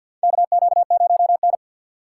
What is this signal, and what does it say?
Morse code, heard as a keyed tone, about 53 wpm, 700 Hz. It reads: KQ0N
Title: USA Call Signs - 6x Character Spacing - 50wpm